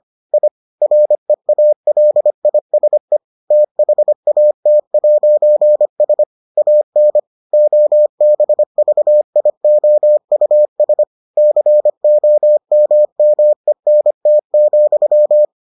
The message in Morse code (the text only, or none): I REALISE THAT'S AN OBVIOUS COMMENT,